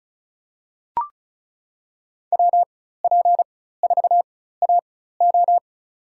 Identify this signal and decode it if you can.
Morse code, heard as a keyed tone, about 36 wpm, spaced as for 16 wpm, 700 Hz. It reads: WP4AO
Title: USA Call Signs - 5 Characters - 4x Character Spacing - 35wpm